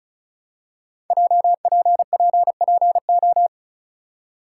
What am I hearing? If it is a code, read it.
Morse code, heard as a keyed tone, about 36 wpm, 700 Hz. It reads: JPPPO
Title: Code Groups - 5 Letters - 35wpm